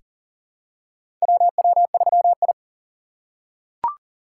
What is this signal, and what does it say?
Morse code, heard as a keyed tone, about 40 wpm, 700 Hz. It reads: WW3I
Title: USA Call Signs - 2x Character Spacing - 40wpm